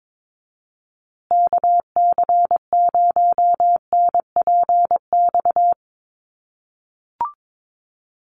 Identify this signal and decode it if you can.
Morse code, heard as a keyed tone, about 22 wpm, 700 Hz. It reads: KC0NPX
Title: USA Call Signs Repeated 2x 22wpm